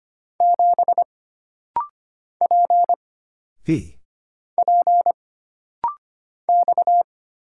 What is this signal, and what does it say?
Morse code, heard as a keyed tone, about 25 wpm, 700 Hz. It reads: 7 P P X
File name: Single Character (Letter:Number:Punctuation:Prosign) - Rapid-Fire - 25wpm